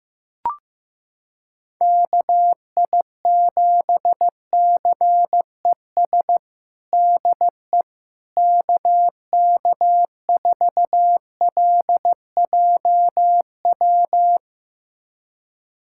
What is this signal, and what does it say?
Morse code, heard as a keyed tone, about 15 wpm, 700 Hz. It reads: KI7CES DE KK4LJW
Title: USA Callsign DE Callsign 15wpm